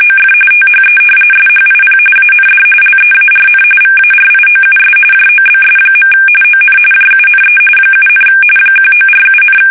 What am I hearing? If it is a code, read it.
RTTY, about 45.45 baud. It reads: OF THE SHIP AND LEARN VARIOUS JOBS. ON JUNE 7, 1947,
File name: fsk850 / battleshipnj